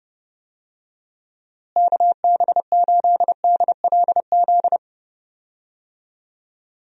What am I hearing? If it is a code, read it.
Morse code, heard as a keyed tone, about 30 wpm, 700 Hz. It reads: KB8DLZ
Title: USA Call Signs - 6 Characters - Repeated 3x - 30wpm